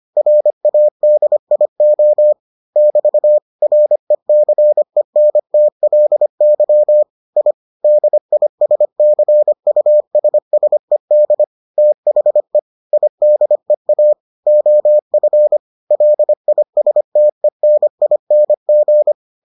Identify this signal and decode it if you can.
Morse code, heard as a keyed tone, about 25 wpm, 600 Hz. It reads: RADIO <BT> RECENTLY I DISCUSSED THE IDEA OF LISTENING